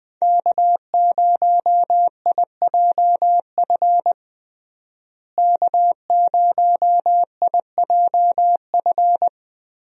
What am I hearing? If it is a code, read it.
Morse code, heard as a keyed tone, about 21 wpm, 700 Hz. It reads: K0IJF K0IJF
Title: USA Call Signs - 5 Characters - Repeated 3x - 20wpm